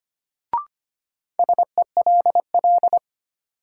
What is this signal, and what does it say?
Morse code, heard as a keyed tone, about 25 wpm, 700 Hz. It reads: SELL
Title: Sentences from Top 1100 Words - New Word Drill Alt - 25wpm